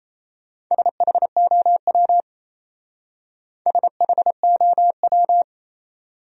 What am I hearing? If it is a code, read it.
Morse code, heard as a keyed tone, about 38 wpm, 700 Hz. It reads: SHOW SHOW